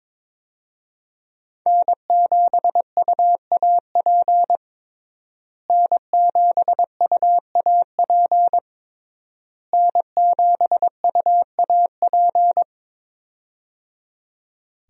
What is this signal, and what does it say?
Morse code, heard as a keyed tone, about 22 wpm, 700 Hz. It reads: N7UAP N7UAP N7UAP
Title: USA Call Signs - 5 Characters - Repeated 3x - 22wpm